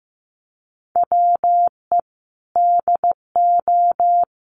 Morse code, heard as a keyed tone, about 15 wpm, 700 Hz. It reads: WE DO
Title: Sets of 4 Words 15wpm